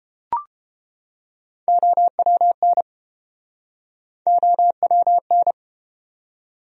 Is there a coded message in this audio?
Yes: Morse code from 1.7 to 5.5 s, heard as a keyed tone, about 29 wpm, 700 Hz. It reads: OWN OWN